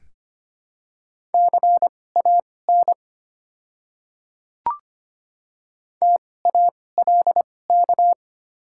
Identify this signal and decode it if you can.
Morse code, heard as a keyed tone, about 25 wpm, spaced as for 18 wpm, 700 Hz. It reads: CAN TALK
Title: 2x Character Spacing - Top 500 Words - 25wpm